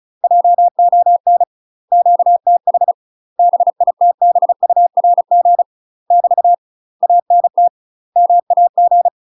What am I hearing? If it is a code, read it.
Morse code, heard as a keyed tone, about 35 wpm, 700 Hz. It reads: JON QTH BITBURG <BT> ANT MAG
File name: UK MASTER QSOs TRAINING/35WPM